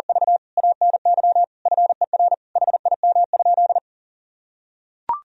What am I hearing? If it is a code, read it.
Morse code, heard as a keyed tone, about 40 wpm, 700 Hz. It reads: V ANY FER HIM?
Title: Sentences from Top 400 Words and Abbreviations 40wpm